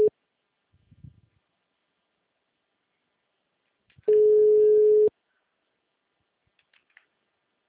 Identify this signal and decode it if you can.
Ringback tone (Europe, 425 Hz).
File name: out- -2011- - - .709